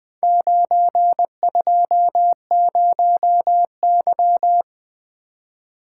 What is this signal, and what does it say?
Morse code, heard as a keyed tone, about 20 wpm, 700 Hz. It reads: 920Y
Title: Four Letter-Number Combinations - 20wpm